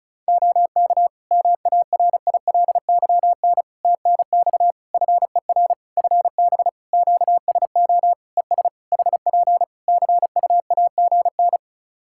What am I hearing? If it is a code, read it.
Morse code, heard as a keyed tone, about 36 wpm, 700 Hz. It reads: OK MARILYN TNX FER FB QSO ES HP CUAGN
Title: Practice QSO - Exchange 4 - Ending - 35wpm